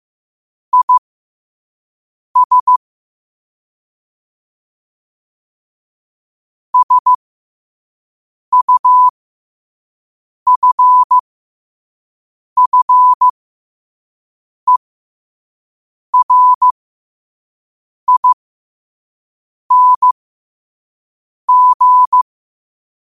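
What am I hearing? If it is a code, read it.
Morse code, heard as a keyed tone, about 15 wpm, spaced as for 5 wpm, 1000 Hz. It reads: IS SUFFERING
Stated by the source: Morse code at 5 wpm.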